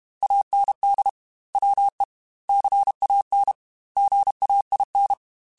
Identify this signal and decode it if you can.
Morse code, heard as a keyed tone, about 32 wpm, 800 Hz. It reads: AND WE CAN GAIN
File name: Technical Subjects / Technical Subjects-mp3s / Sw Pwr Sup @30